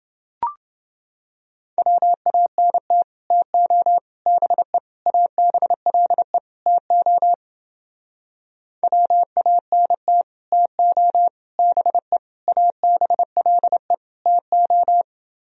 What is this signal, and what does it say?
Morse code, heard as a keyed tone, about 29 wpm, 700 Hz. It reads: WANT TO BE ABLE TO WANT TO BE ABLE TO